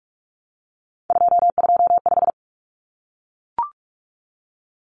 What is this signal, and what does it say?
Morse code, heard as a keyed tone, about 44 wpm, 700 Hz. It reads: 225
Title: Common Cut Numbers - 3-Digit 45wpm